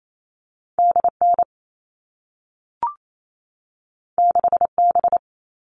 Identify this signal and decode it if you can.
Morse code, heard as a keyed tone, about 28 wpm, 700 Hz. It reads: DN 6B
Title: All Cut Numbers - 2-Digit 28wpm